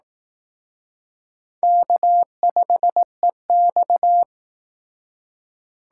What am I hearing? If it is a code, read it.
Morse code, heard as a keyed tone, about 19 wpm, 700 Hz. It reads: K5EX